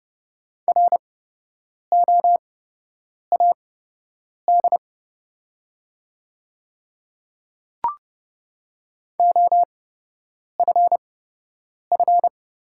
Morse code, heard as a keyed tone, about 30 wpm, spaced as for 8 wpm, 700 Hz. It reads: ROAD OFF
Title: 8x Character Spacing - Top 500 Words - 30wpm